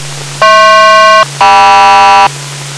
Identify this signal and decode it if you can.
Aircraft SELCAL, heard as two pairs of tones. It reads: HP-KM